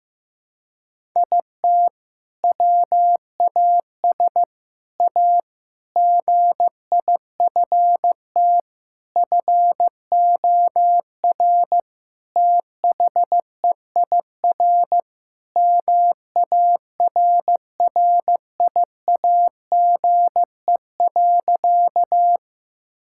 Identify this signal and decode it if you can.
Morse code, heard as a keyed tone, about 15 wpm, 700 Hz. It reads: IT WAS A GIFT FOR THEIR MARRIAGE.
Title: Sentences from Top 1000 Words - Review Part 6 - 15wpm